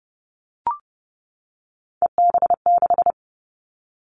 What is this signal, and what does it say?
Morse code, heard as a keyed tone, about 32 wpm, 700 Hz. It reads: EB6
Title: Callsign Prefixes for Top 50 DXCC - 30wpm